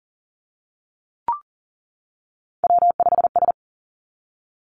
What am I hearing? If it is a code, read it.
Morse code, heard as a keyed tone, about 39 wpm, 700 Hz. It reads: W5S
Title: Three Challenging Letter-Number Combinations 40wpm